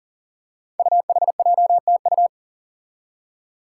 Morse code, heard as a keyed tone, about 42 wpm, 700 Hz. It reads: UHJTU
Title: Five Challenging Letter-Number Combinations 40wpm